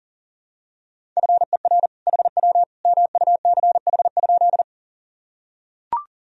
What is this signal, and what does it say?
Morse code, heard as a keyed tone, about 40 wpm, 700 Hz. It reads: FER HW MUCH?
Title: Sentences from Top 200 Words and Abbreviations 40wpm